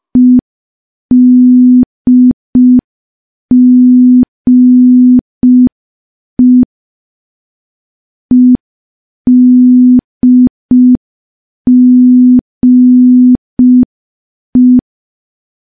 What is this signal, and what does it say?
Morse code, heard as a keyed tone, about 5 wpm, 250 Hz. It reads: EDGE EDGE
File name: morse / morse-training-05WPM